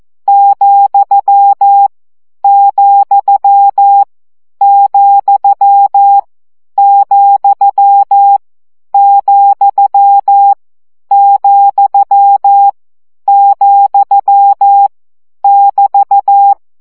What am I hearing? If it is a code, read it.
Morse code, heard as a keyed tone, about 14 wpm, 800 Hz. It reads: , , , , , , , <BT>
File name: CW gyakorlas vesszo 12 baud